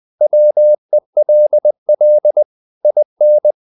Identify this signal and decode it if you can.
Morse code, heard as a keyed tone, about 20 wpm, 600 Hz. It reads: WELL IN
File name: the old man and the sea part 2 20wpm